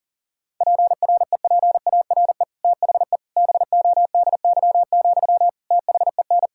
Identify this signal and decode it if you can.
Morse code, heard as a keyed tone, about 40 wpm, 700 Hz. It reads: PREPARE THE BODY, THEN
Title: Sentences from Top 1000 Words - Review Part 6 - 40wpm